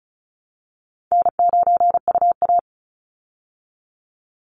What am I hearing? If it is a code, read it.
Morse code, heard as a keyed tone, about 34 wpm, 700 Hz. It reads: N9UA
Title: Call Signs Repeated 3x - 35wpm